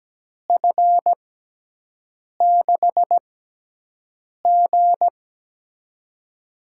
Morse code, heard as a keyed tone, about 17 wpm, spaced as for 6 wpm, 700 Hz. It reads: F6G
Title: USA Call Signs - 4 Characters - 6x Character Spacing - 17wpm